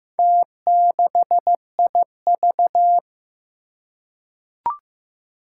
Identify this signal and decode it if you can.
Morse code, heard as a keyed tone, about 15 wpm, 700 Hz. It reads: T6IV
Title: Four Challenging Letter-Number Combinations 15wpm